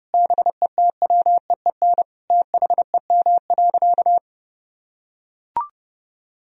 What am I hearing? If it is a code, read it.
Morse code, heard as a keyed tone, about 29 wpm, 700 Hz. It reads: BETWEEN THEM.
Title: Sentences from Top 1000 Words - Review Part 1 - 30wpm